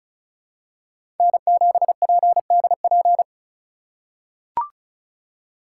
Morse code, heard as a keyed tone, about 36 wpm, 700 Hz. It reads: N7PDP
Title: USA Call Signs - 2x Character Spacing - 35wpm